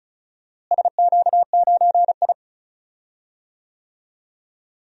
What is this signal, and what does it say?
Morse code, heard as a keyed tone, about 34 wpm, 700 Hz. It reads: SQ9I
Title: DXCC-100 Call Signs - 4 Characters - Repeated 3x - 35wpm